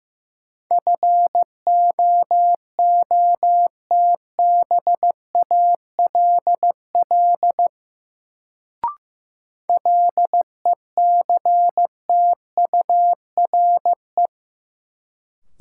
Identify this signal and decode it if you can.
Morse code, heard as a keyed tone, about 15 wpm, 700 Hz. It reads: FOOTBALL LECTURE
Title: Sentences from Top 1600 Words - New Word Drill - 15wpm